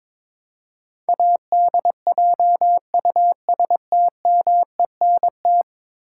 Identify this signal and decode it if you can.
Morse code, heard as a keyed tone, about 22 wpm, 700 Hz. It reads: ADJUSTMENT
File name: practice / 22wpm / 10 Letter Words - First 3 Letters Given - 22wpm